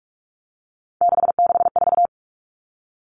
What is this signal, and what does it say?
Morse code, heard as a keyed tone, about 44 wpm, 700 Hz. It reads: 664
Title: Common Cut Numbers - 3-Digit 45wpm